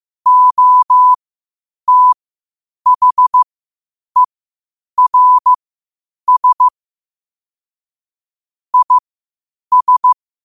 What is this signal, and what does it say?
Morse code, heard as a keyed tone, about 15 wpm, spaced as for 8 wpm, 1000 Hz. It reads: OTHERS IS